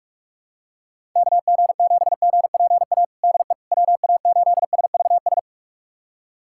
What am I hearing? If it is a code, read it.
Morse code, heard as a keyed tone, about 41 wpm, 700 Hz. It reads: KG7GPA DE WA8SVS
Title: USA Callsign DE Callsign 45wpm